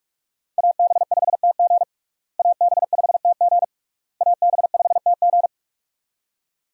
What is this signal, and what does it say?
Morse code, heard as a keyed tone, about 42 wpm, 700 Hz. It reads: AB5TG AB5TG AB5TG
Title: USA Call Signs - 5 Characters - Repeated 3x - 45wpm